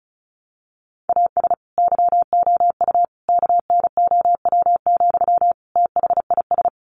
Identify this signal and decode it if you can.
Morse code, heard as a keyed tone, about 34 wpm, 700 Hz. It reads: AS YOU KNOW, THIS
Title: Sentences from Top 1100 Words 35wpm